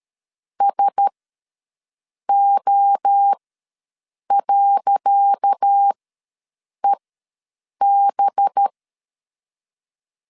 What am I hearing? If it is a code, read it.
Morse code, heard as a keyed tone, about 12 wpm, spaced as for 6 wpm, 800 Hz. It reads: SO.EB
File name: LEZIONI / 10 SIGNS / 10C DOT ws 12wpm 3548